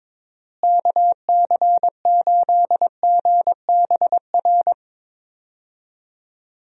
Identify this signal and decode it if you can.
Morse code, heard as a keyed tone, about 21 wpm, 700 Hz. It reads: KC8GBR
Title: USA Call Signs - 6 Characters - Repeated 3x - 22wpm